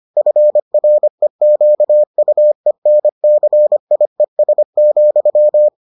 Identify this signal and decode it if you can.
Morse code, heard as a keyed tone, about 25 wpm, 600 Hz. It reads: FREQUENCIES,